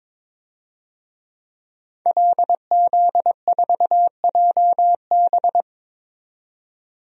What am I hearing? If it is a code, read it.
Morse code, heard as a keyed tone, about 23 wpm, 700 Hz. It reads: LZ4JB
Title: DXCC-100 Call Signs - 5 Characters - 22wpm